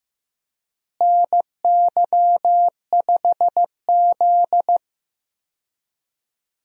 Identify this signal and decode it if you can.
Morse code, heard as a keyed tone, about 15 wpm, 700 Hz. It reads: NY5Z
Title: USA Call Signs - 4 Characters - Repeated 3x - 15wpm